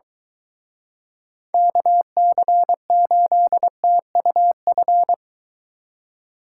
Morse code, heard as a keyed tone, about 23 wpm, 700 Hz. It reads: KC8TUF